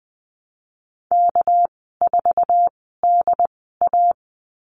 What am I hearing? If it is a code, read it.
Morse code, heard as a keyed tone, about 20 wpm, spaced as for 14 wpm, 700 Hz. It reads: K4DA
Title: 207 Lesson 33g - Letter j - Call Signs 20wpm